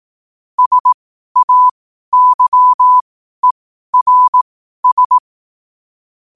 Morse code, heard as a keyed tone, about 18 wpm, spaced as for 12 wpm, 1000 Hz. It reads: SAYERS
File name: qotdfiles / 12wpm